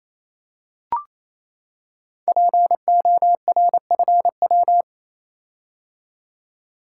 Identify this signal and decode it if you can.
Morse code, heard as a keyed tone, about 27 wpm, 700 Hz. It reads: PORFW
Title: Code Groups - 5 Letters - 28wpm